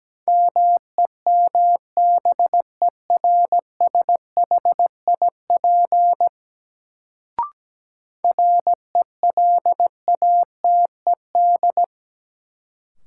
Morse code, heard as a keyed tone, about 17 wpm, 700 Hz. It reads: MEMBERSHIP RELATED
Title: Sentences from Top 1400 Words - New Word Drill Alt - 17wpm